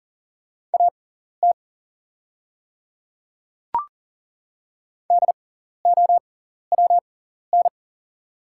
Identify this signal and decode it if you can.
Morse code, heard as a keyed tone, about 40 wpm, spaced as for 14 wpm, 700 Hz. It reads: AT DOWN
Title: 6x Character Spacing - Top 200 Words - 40wpm